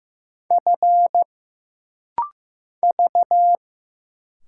Morse code, heard as a keyed tone, about 15 wpm, 700 Hz. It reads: F V
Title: Single Letters - Rapid-Fire 15wpm